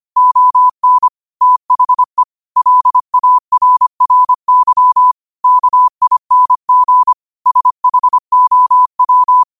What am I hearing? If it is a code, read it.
Morse code, heard as a keyed tone, about 25 wpm, 1000 Hz. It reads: ON THE LARRY KING SHOW
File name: qotdfiles / 25wpm